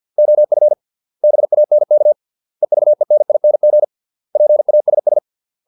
Morse code, heard as a keyed tone, about 50 wpm, 600 Hz. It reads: OF BANK EVENING PASS